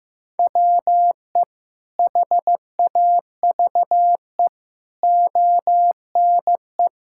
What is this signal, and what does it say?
Morse code, heard as a keyed tone, about 15 wpm, 700 Hz. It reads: WE HAVE ONE
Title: Sentences from Top 1000 Words - Review Part 6 - 15wpm